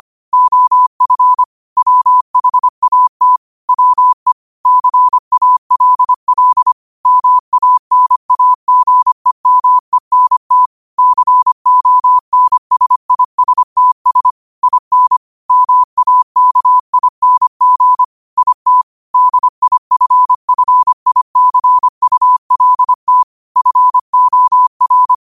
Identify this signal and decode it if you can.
Morse code, heard as a keyed tone, about 25 wpm, 1000 Hz. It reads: OF WHAT WE CALL MANAGEMENT CONSISTS IN MAKING IT DIFFICULT FOR